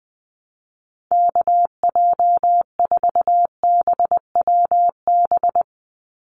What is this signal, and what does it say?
Morse code, heard as a keyed tone, about 20 wpm, 700 Hz. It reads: KJ4BWB
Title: USA Call Signs 20wpm